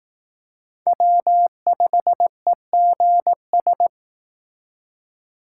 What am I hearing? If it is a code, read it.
Morse code, heard as a keyed tone, about 18 wpm, 700 Hz. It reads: W5EGS
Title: Speed Racing - USA 1x3 Call Signs 15wpm